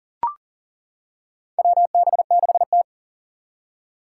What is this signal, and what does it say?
Morse code, heard as a keyed tone, about 40 wpm, 700 Hz. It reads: WB6T
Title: Four Challenging Letter-Number Combinations 40wpm